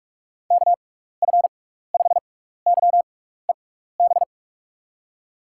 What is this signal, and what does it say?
Morse code, heard as a keyed tone, about 46 wpm, spaced as for 16 wpm, 700 Hz. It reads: KF5YEB
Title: USA Call Signs - 6 Characters - 6x Character Spacing - 45wpm